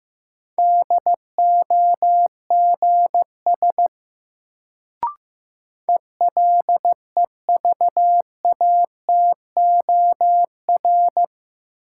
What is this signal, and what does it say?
Morse code, heard as a keyed tone, about 15 wpm, 700 Hz. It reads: DOGS ELEVATOR
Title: Sentences from Top 1600 Words - New Word Drill - 15wpm